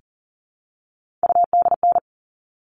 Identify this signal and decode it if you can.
Morse code, heard as a keyed tone, about 40 wpm, 700 Hz. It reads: UDN
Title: All Cut Numbers - 3-Digit 40wpm